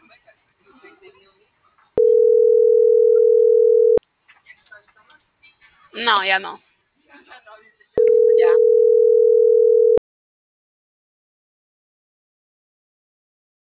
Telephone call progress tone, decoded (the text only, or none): ringback tone